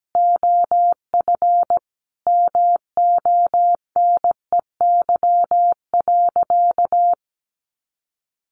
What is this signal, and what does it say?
Morse code, heard as a keyed tone, about 17 wpm, 700 Hz. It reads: OF MONEY.
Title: Sentences from Top 1300 Words 17wpm